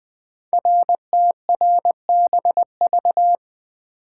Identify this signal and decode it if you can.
Morse code, heard as a keyed tone, about 20 wpm, 700 Hz. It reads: RTRBV
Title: Code Groups - 5 Letters - 20wpm